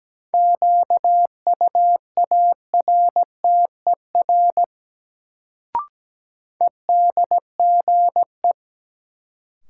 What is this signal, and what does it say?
Morse code, heard as a keyed tone, about 17 wpm, 700 Hz. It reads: QUARTER EDGE
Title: Sentences from Top 1300 Words - New Word Drill Alt - 17wpm